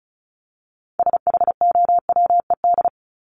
Morse code, heard as a keyed tone, about 34 wpm, 700 Hz. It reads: SHOWED
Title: ED Suffix - Top 100 Words with Base Words 35wpm